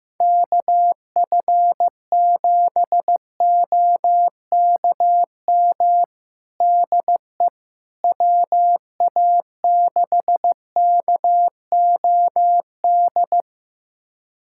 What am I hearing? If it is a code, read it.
Morse code, heard as a keyed tone, about 15 wpm, 700 Hz. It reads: KF7OKM DE WA6KOD
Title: USA Callsign DE Callsign - 6 Characters - 15wpm